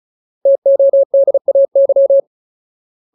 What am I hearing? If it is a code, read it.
Morse code, heard as a keyed tone, about 36 wpm, 550 Hz. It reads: TODAY
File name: Ultimate Word List / 35WPM Ultimate